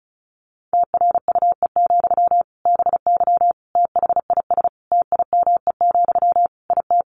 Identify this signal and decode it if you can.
Morse code, heard as a keyed tone, about 34 wpm, 700 Hz. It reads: TRUE, BY THIS TIME, IT
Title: Sentences from Top 1500 Words 35wpm